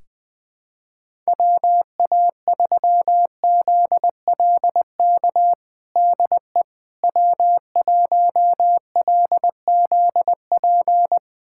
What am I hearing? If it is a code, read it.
Morse code, heard as a keyed tone, about 20 wpm, 700 Hz. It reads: WA3ZLK DE W1LZP
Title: USA Callsign DE Callsign 20wpm